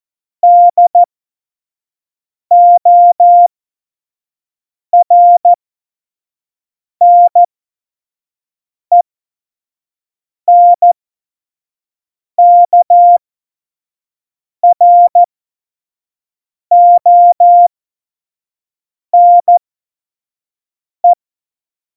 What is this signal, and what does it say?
Morse code, heard as a keyed tone, about 14 wpm, spaced as for 5 wpm, 700 Hz. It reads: DORNENKRONE
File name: Morsekurs / Witze / Witz 05. Januar 2021, 25 F70